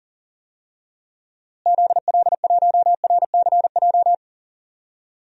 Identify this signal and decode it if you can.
Morse code, heard as a keyed tone, about 40 wpm, 700 Hz. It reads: ZL1RCJ
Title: DXCC-100 Call Signs - 6 Characters - Repeated 2x - 40wpm